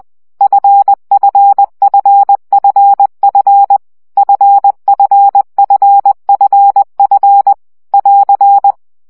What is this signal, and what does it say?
Morse code, heard as a keyed tone, about 20 wpm, 800 Hz. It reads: FFFFF FFFFF <AR>
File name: CW gyakorlas f 17 baud